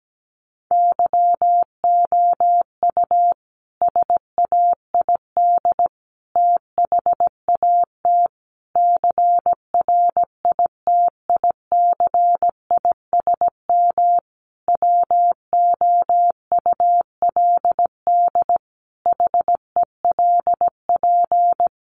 Morse code, heard as a keyed tone, about 17 wpm, 700 Hz. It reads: YOU SAID THAT CRITICISM WOULD HELP
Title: Sentences from Top 1300 Words 17wpm